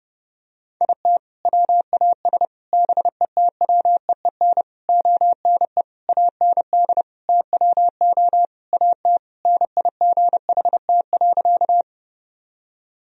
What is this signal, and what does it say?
Morse code, heard as a keyed tone, about 32 wpm, 700 Hz. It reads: IT WAS BETWEEN ONE AND TWO AT NIGHT.
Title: Sentences from Top 1000 Words - Review Part 1 - 30wpm